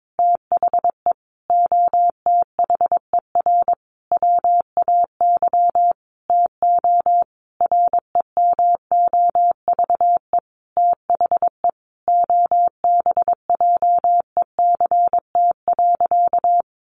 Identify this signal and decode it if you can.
Morse code, heard as a keyed tone, about 22 wpm, 700 Hz. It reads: THE OTHER WAY TO REMOVE THE OBJECT.
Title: Sentences from Top 800 Words 22wpm